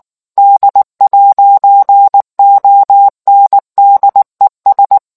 Morse code, heard as a keyed tone, about 19 wpm, 800 Hz. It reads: D'ONDES